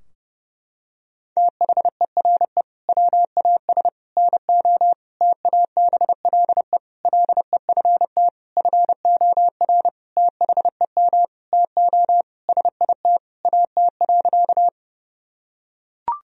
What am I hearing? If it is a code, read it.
Morse code, heard as a keyed tone, about 30 wpm, 700 Hz. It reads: THERE WAS NO TABLE LEFT FOR THEM TO SIT AT.
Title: Sentences from Top 1000 Words - Review Part 2 - 30wpm